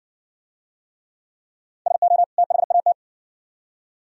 Morse code, heard as a keyed tone, about 60 wpm, 700 Hz. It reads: SO THAT